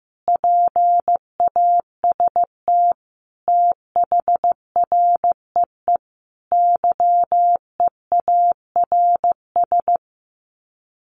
Morse code, heard as a keyed tone, about 15 wpm, 700 Hz. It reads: PAST THREE YEARS
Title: Sets of 5 Words Encore2 15wpm